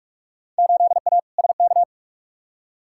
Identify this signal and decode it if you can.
Morse code, heard as a keyed tone, about 46 wpm, 700 Hz. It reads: 8A SK
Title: Field Day Exchanges Encore - 45wpm